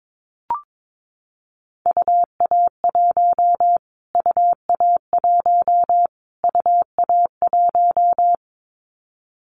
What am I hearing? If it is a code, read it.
Morse code, heard as a keyed tone, about 22 wpm, 700 Hz. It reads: UA1 UA1 UA1
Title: Callsign Prefixes for Top 25 DXCC - Repeated 3x - 22wpm